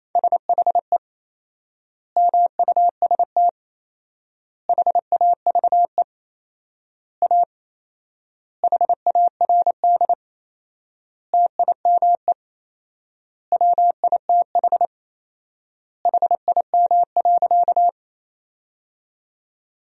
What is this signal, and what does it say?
Morse code, heard as a keyed tone, about 28 wpm, 700 Hz. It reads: SHE MUST HAVE A HARD TIME WITH HIM.
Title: 4x Word-Spacing - Sentences from Top 300 Words - 28wpm